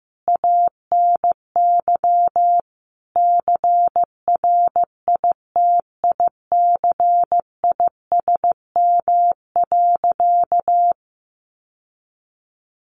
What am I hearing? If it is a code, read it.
Morse code, heard as a keyed tone, about 15 wpm, 700 Hz. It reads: ANY CRITICISM.
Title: Sentences from Top 1300 Words 15wpm